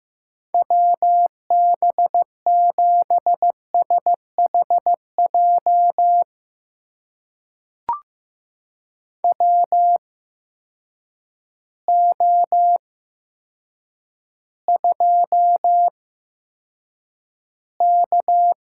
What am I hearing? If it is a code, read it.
Morse code, heard as a keyed tone, about 15 wpm, spaced as for 8 wpm, 700 Hz. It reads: WB7SHJ WO2K
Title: USA Call Signs - 8x Character Spacing - 15wpm